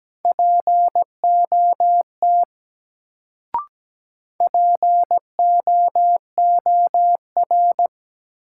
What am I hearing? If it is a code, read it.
Morse code, heard as a keyed tone, about 17 wpm, 700 Hz. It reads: POT POOR
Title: Sentences from Top 1000 Words - New Word Drill - 17wpm